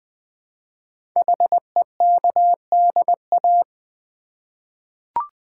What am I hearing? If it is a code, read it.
Morse code, heard as a keyed tone, about 20 wpm, 700 Hz. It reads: HEKDA
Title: Code Groups - 5 Letters - 20wpm